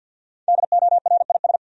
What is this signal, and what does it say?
Morse code, heard as a keyed tone, about 47 wpm, 700 Hz. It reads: DORIS
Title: USA Common Names - Max 5 Letters - 50wpm